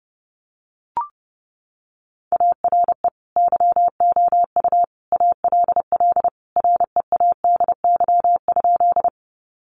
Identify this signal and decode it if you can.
Morse code, heard as a keyed tone, about 30 wpm, 700 Hz. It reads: ARE YOU ALL READY?
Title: Sentences from Top 400 Words 30wpm